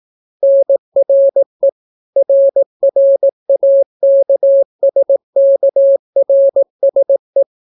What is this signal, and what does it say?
Morse code, heard as a keyed tone, about 18 wpm, 550 Hz. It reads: NRE RRAKSKRSE